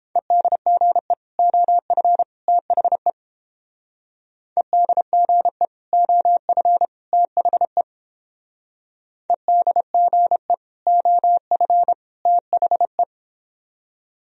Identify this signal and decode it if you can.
Morse code, heard as a keyed tone, about 31 wpm, 700 Hz. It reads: EDGE OF THE EDGE OF THE EDGE OF THE